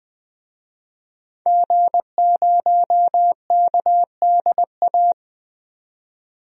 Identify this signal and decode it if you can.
Morse code, heard as a keyed tone, about 21 wpm, 700 Hz. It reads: G0KDA
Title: DXCC-100 Call Signs - 5 Characters - 20wpm